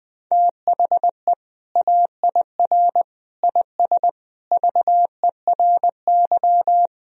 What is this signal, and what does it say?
Morse code, heard as a keyed tone, about 21 wpm, 700 Hz. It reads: THE AIR IS VERY
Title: Sentences from Top 1000 Words - Review Part 1 - 20wpm